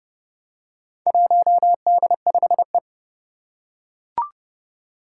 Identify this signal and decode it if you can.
Morse code, heard as a keyed tone, about 30 wpm, 700 Hz. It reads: 1D5E
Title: Four Challenging Letter-Number Combinations 30wpm